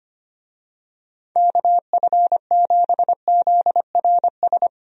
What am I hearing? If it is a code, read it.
Morse code, heard as a keyed tone, about 25 wpm, 700 Hz. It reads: KF7ZRS
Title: USA Call Signs - 6 Characters - 25wpm